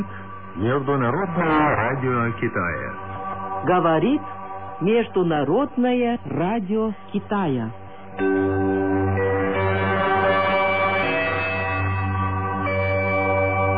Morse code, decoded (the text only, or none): none